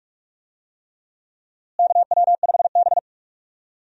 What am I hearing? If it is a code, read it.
Morse code, heard as a keyed tone, about 47 wpm, 700 Hz. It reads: KW5B
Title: USA Call Signs - 4 Characters - Repeated 3x - 45wpm